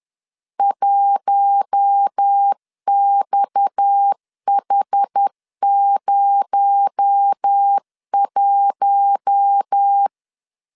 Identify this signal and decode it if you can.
Morse code, heard as a keyed tone, about 10 wpm, 800 Hz. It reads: 1XH01